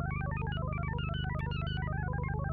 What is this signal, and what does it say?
There is no code.